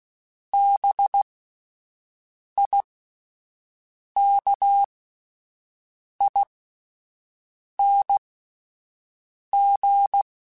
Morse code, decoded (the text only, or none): BIKING